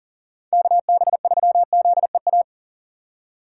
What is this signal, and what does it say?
Morse code, heard as a keyed tone, about 39 wpm, 700 Hz. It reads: KB3ZEA
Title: USA Call Signs - 6 Characters - 3x Character Spacing - 40wpm